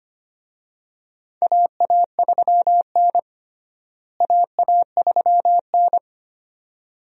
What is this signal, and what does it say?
Morse code, heard as a keyed tone, about 25 wpm, 700 Hz. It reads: AA3N AA3N
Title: USA Call Signs - 4 Characters - Repeated 2x - 25wpm